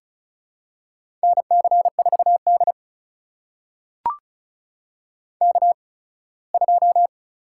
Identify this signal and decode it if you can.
Morse code, heard as a keyed tone, about 36 wpm, spaced as for 21 wpm, 700 Hz. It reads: NC4D K2
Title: USA Call Signs - 8x Character Spacing - 35wpm